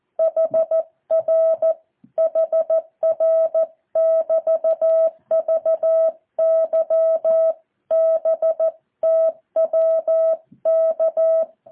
Morse code, heard as a keyed tone, about 14 wpm, 650 Hz. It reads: HR HR <BT> VYBTWK
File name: jh8cbh / 2023cw